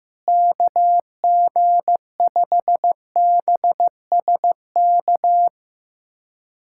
Morse code, heard as a keyed tone, about 15 wpm, 700 Hz. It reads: KG5BSK
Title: USA Callsign DE Callsign 15wpm